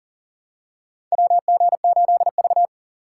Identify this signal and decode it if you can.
Morse code, heard as a keyed tone, about 40 wpm, 700 Hz. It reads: WG8V
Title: USA Call Signs - 6x Character Spacing - 40wpm